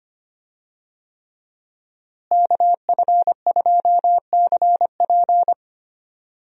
Morse code, heard as a keyed tone, about 25 wpm, 700 Hz. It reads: KF2CP
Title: USA Call Signs - 5 Characters - Repeated 3x - 25wpm